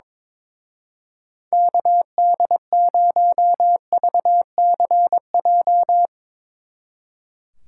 Morse code, heard as a keyed tone, about 22 wpm, 700 Hz. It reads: KD0VCJ
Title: Speed Racing - USA 2x3 Call Signs 22wpm